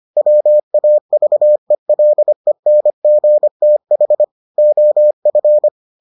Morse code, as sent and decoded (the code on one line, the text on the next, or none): .-- .- ...- . .-.. -. --. - .... / --- ..-.
WAVELENGTH OF